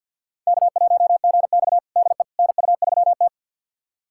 Morse code, heard as a keyed tone, about 52 wpm, 700 Hz. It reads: K1GX DE NU3T
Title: USA Callsign DE Callsign - 4 Characters - 50wpm